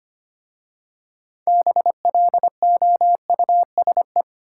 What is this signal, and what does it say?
Morse code, heard as a keyed tone, about 25 wpm, 700 Hz. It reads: BLOUSE
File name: practice / 25wpm / Difficult Words with Hints - 25wpm